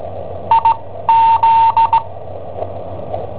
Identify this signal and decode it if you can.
Morse code, heard as a keyed tone, about 17 wpm, spaced as for 13 wpm, 910 Hz. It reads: IZ
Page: Cassetta esercizi preparata da I7OHP per IW5ELP